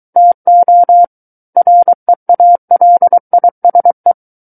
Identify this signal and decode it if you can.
Morse code, heard as a keyed tone, about 23 wpm, 700 Hz. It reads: TO REALISE
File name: Book One / 14-Chapter 14 - 23 WPM